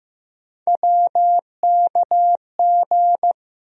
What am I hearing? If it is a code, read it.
Morse code, heard as a keyed tone, about 15 wpm, 700 Hz. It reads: WKG
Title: USA Call Sign Postfixes - 3 Letters 15wpm